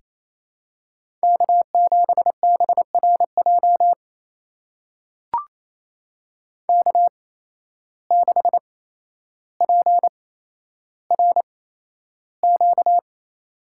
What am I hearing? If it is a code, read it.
Morse code, heard as a keyed tone, about 28 wpm, spaced as for 12 wpm, 700 Hz. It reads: K7BRJ K6PRQ
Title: USA Call Signs - 5 Characters - 8x Character Spacing - 28wpm